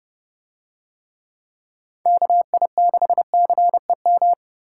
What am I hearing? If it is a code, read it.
Morse code, heard as a keyed tone, about 29 wpm, 700 Hz. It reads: KI6CEM